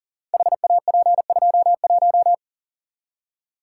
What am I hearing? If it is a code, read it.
Morse code, heard as a keyed tone, about 40 wpm, 700 Hz. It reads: HAP21
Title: Code Groups - 5 Letter:Numbers - 40wpm